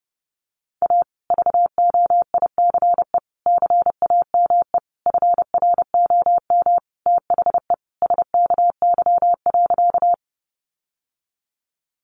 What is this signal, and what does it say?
Morse code, heard as a keyed tone, about 30 wpm, 700 Hz. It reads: A VOICE CAME FROM THE SKY.
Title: Sentences from Top 1200 Words 30wpm